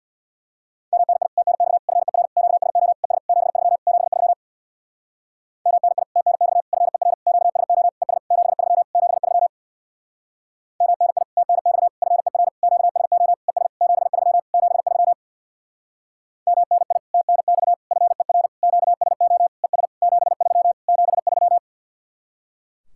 Morse code, heard as a keyed tone, about 63 wpm, 700 Hz. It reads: MNI TNX FER QSO ES 73 73 MNI TNX FER QSO ES 73 73 MNI TNX FER QSO ES 73 73 MNI TNX FER QSO ES 73 73